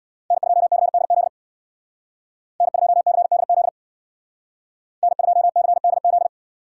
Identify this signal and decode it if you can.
Morse code, heard as a keyed tone, about 64 wpm, 700 Hz. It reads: N2CDZ N2CDZ N2CDZ